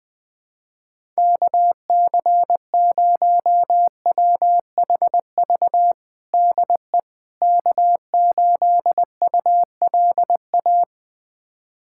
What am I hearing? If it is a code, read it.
Morse code, heard as a keyed tone, about 20 wpm, 700 Hz. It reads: KC0WHV DE K8ULA
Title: USA Callsign DE Callsign 20wpm